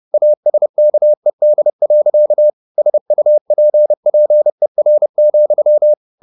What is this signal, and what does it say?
Morse code, heard as a keyed tone, about 30 wpm, 600 Hz. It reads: ASKED. SUPPER,
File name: the old man and the sea / the old man and the sea 1 30wpm